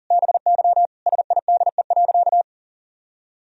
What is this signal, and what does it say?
Morse code, heard as a keyed tone, about 40 wpm, 700 Hz. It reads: BY SIDE.
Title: Sentences from Top 1000 Words - Review Part 2 - 40wpm